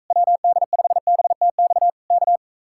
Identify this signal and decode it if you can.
Morse code, heard as a keyed tone, about 44 wpm, 700 Hz. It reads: WD5BTX K